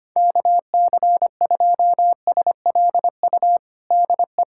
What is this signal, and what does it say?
Morse code, heard as a keyed tone, about 25 wpm, 700 Hz. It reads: KC2SLU DE
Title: USA Callsign DE Callsign - 6 Characters - 25wpm